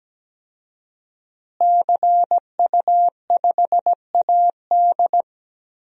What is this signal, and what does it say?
Morse code, heard as a keyed tone, about 17 wpm, 700 Hz. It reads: CU5AD
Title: DXCC-100 Call Signs - 5 Characters - Repeated 2x - 17wpm